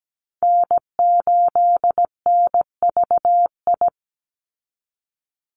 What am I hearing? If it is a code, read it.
Morse code, heard as a keyed tone, about 17 wpm, 700 Hz. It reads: N8NVI
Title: USA Call Signs 17wpm